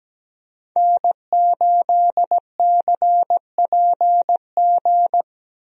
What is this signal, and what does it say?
Morse code, heard as a keyed tone, about 17 wpm, 700 Hz. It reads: N8CPG
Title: USA Call Signs - 5 Characters - Repeated 3x - 17wpm